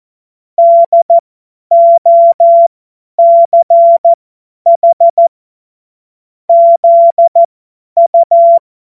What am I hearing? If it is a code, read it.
Morse code, heard as a keyed tone, about 14 wpm, spaced as for 10 wpm, 680 Hz. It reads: DOCH ZU
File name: Morsekurs / Witze / Witz 23. Dezember 2025 50 F70